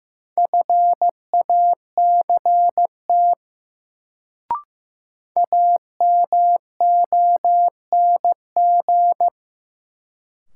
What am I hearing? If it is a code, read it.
Morse code, heard as a keyed tone, about 15 wpm, 700 Hz. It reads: FACT AMONG
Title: Sets of 2 Words - New Word Drill - 15wpm